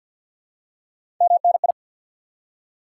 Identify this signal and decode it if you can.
Morse code, heard as a keyed tone, about 50 wpm, 700 Hz. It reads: MNI
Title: Top 90 QSO Elements 50wpm